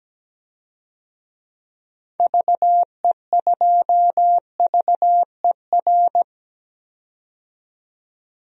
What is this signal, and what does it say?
Morse code, heard as a keyed tone, about 17 wpm, 700 Hz. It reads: VE2VER
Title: DXCC-100 Call Signs - 6 Characters - 17wpm